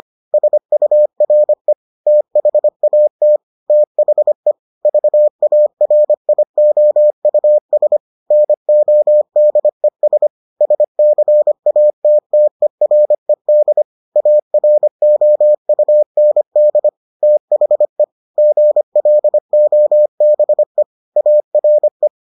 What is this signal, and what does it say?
Morse code, heard as a keyed tone, about 25 wpm, 600 Hz. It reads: SURE THAT THE VARIOUS NODES SCATTERED AROUND THE GLOBE ARE